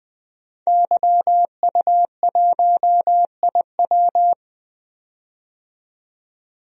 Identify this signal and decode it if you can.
Morse code, heard as a keyed tone, about 20 wpm, 700 Hz. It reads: YU1IW
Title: DXCC-100 Call Signs - 5 Characters - 20wpm